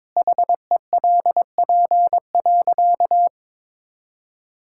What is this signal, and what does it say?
Morse code, heard as a keyed tone, about 22 wpm, 700 Hz. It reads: HELP.
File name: practice / 22wpm / Sentences with 2-Word Phrases - 22wpm